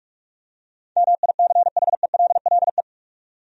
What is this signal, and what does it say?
Morse code, heard as a keyed tone, about 46 wpm, 700 Hz. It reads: MICHELLE
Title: USA Common Names - 45wpm